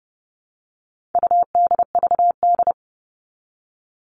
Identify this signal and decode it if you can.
Morse code, heard as a keyed tone, about 30 wpm, 700 Hz. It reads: UDVD
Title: All Cut Numbers - 4-Digit 30wpm